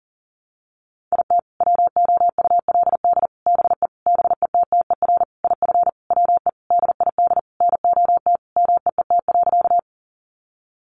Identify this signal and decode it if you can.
Morse code, heard as a keyed tone, about 40 wpm, 700 Hz. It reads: IT WOULD BE BETTER IF WE DID NOT MEET.
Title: Sentences from Top 1100 Words 40wpm